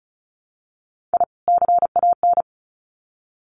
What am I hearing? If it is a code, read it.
Morse code, heard as a keyed tone, about 34 wpm, 700 Hz. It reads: I CAN
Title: Sets of 2 Words 35wpm